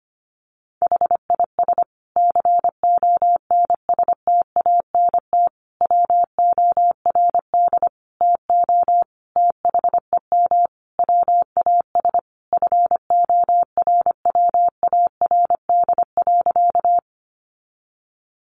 Morse code, heard as a keyed tone, about 25 wpm, 700 Hz. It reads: HIS CONSTANT WORD TO THEM WAS FORWARD.
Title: Sentences from Top 1500 Words 25wpm